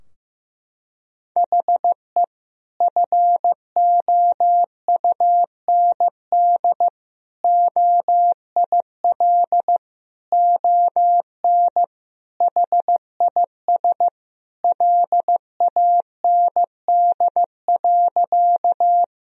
Morse code, heard as a keyed tone, about 15 wpm, 700 Hz. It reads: HE FOUND OIL ON HIS LAND.
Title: Sentences from Top 1000 Words - Review Part 4 - 15wpm